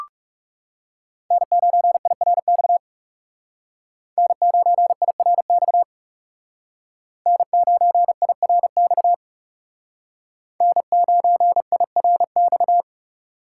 Morse code, heard as a keyed tone, about 36 wpm, 700 Hz. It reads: N9IRX N9IRX N9IRX N9IRX
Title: Speed Racing - USA 1x3 Call Signs 30wpm